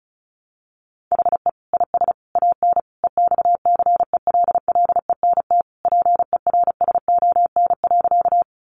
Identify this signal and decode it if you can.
Morse code, heard as a keyed tone, about 34 wpm, 700 Hz. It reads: HE IS AN EXCELLENT PERSON.
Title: Sentences from Top 1000 Words 35wpm